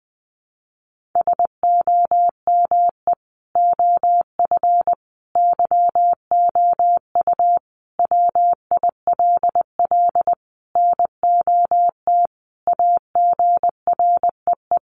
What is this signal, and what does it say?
Morse code, heard as a keyed tone, about 20 wpm, 700 Hz. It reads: SOME OF YOU WILL NOT AGREE
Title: Sentences from Top 1100 Words 20wpm